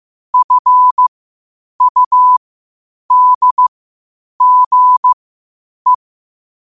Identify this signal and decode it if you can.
Morse code, heard as a keyed tone, about 15 wpm, spaced as for 8 wpm, 1000 Hz. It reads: FUDGE
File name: qotdfiles / 8wpm